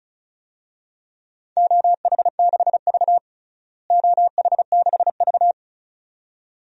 Morse code, heard as a keyed tone, about 36 wpm, 700 Hz. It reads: OH6V OH6V
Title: DXCC-100 Call Signs - 4 Characters - Repeated 2x - 35wpm